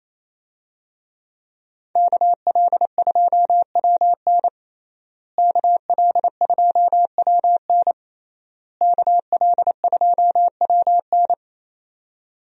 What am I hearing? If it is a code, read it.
Morse code, heard as a keyed tone, about 28 wpm, 700 Hz. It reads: KL2WN KL2WN KL2WN
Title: USA Call Signs - 5 Characters - Repeated 3x - 28wpm